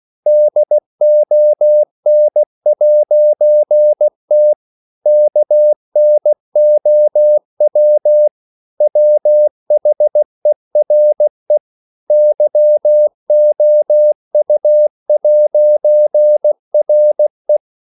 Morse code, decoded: DON'T KNOW WHERE YOU'RE